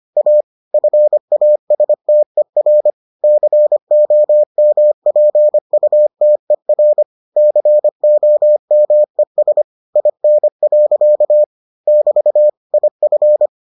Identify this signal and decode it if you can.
Morse code, heard as a keyed tone, about 25 wpm, 600 Hz. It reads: A FASTER COMPUTER COMES IN. <BT> IF